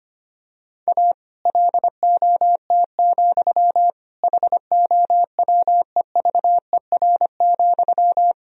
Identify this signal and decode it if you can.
Morse code, heard as a keyed tone, about 25 wpm, 700 Hz. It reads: A LOT, HOWEVER,
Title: Sentences from Top 1000 Words - Review Part 3 - 25wpm